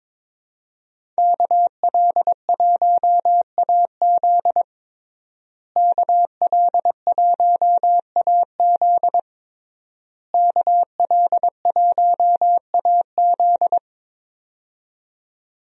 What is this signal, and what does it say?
Morse code, heard as a keyed tone, about 22 wpm, 700 Hz. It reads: KL1AZ KL1AZ KL1AZ
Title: USA Call Signs - 5 Characters - Repeated 3x - 22wpm